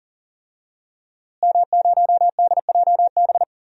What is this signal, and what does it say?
Morse code, heard as a keyed tone, about 37 wpm, 700 Hz. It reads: M0DJB
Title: DXCC-100 Call Signs - 5 Characters - Repeated 3x - 40wpm